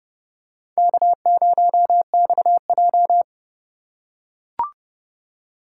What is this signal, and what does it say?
Morse code, heard as a keyed tone, about 29 wpm, 700 Hz. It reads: K0XJ